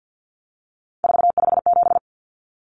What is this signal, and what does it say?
Morse code, heard as a keyed tone, about 50 wpm, 700 Hz. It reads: 457
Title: Common Cut Numbers - 3-Digit 50wpm